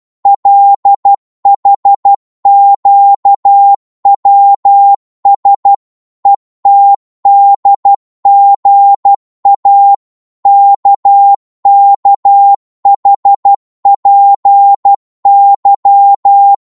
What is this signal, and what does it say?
Morse code, heard as a keyed tone, about 12 wpm, 800 Hz. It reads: LHQWS ETDGA KKHPY
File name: Morse / ltr-12wpm